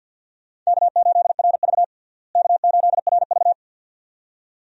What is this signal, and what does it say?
Morse code, heard as a keyed tone, about 47 wpm, 700 Hz. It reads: K8RV K8RV
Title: USA Call Signs - 4 Characters - Repeated 2x - 50wpm